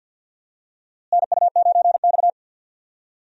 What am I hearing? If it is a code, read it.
Morse code, heard as a keyed tone, about 52 wpm, 700 Hz. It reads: NU9X
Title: USA Call Signs - 4 Characters - Repeated 3x - 50wpm